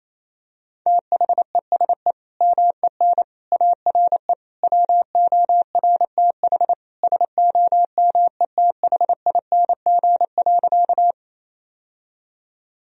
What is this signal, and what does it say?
Morse code, heard as a keyed tone, about 28 wpm, 700 Hz. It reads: THESE MEN ARE WORTH SOMETHING.
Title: Sentences from Top 1000 Words - Review Part 1 - 28wpm